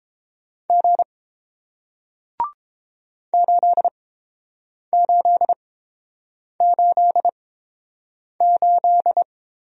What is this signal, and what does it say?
Morse code, heard as a keyed tone, about 30 wpm, spaced as for 7 wpm, 700 Hz. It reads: G 8888